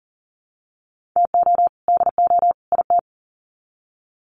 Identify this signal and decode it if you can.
Morse code, heard as a keyed tone, about 40 wpm, 700 Hz. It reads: TO DO IT
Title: Sets of 3 Words 40wpm